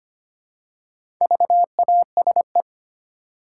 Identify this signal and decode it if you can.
Morse code, heard as a keyed tone, about 25 wpm, 700 Hz. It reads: VASE
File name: practice / 25wpm / Easy Words with Hints - 25wpm